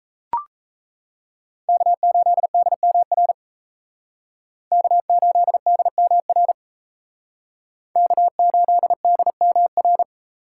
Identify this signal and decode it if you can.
Morse code, heard as a keyed tone, about 36 wpm, 700 Hz. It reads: K8DMR K8DMR K8DMR